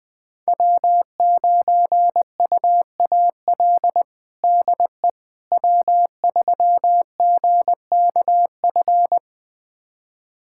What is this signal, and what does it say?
Morse code, heard as a keyed tone, about 21 wpm, 700 Hz. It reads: W9UAL DE W3GKF
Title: USA Callsign DE Callsign - 5 Characters - 20wpm